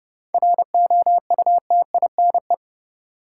Morse code, heard as a keyed tone, about 31 wpm, 700 Hz. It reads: ROUTINE
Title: Seven Letter Words - 30wpm